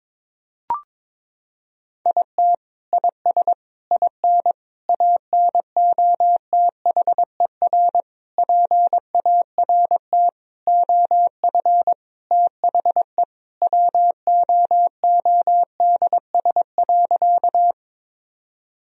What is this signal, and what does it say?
Morse code, heard as a keyed tone, about 22 wpm, 700 Hz. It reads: IT IS IN ANOTHER PART OF THE WOODS.
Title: Sentences from Top 1000 Words - Review Part 2 - 22wpm